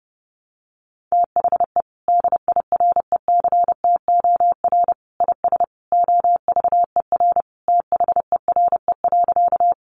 Morse code, heard as a keyed tone, about 28 wpm, 700 Hz. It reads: THE DIRECTOR IS OVER THERE.
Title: Sentences from Top 1300 Words 30wpm